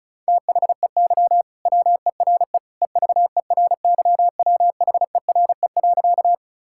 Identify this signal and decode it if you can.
Morse code, heard as a keyed tone, about 34 wpm, 700 Hz. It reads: THEY WERE EVERYWHERE.
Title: Sentences from Top 1600 Words 35wpm